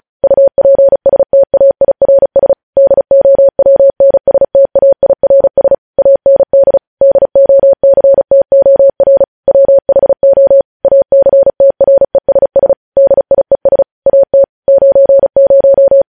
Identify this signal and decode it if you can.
Morse code, heard as a keyed tone, about 36 wpm, 550 Hz. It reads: UPSTAIRS DOWNSTAIRS AND DOCTOR WHO ACTRESS DIES AT 90
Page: news headlines in morse code at 35 wpm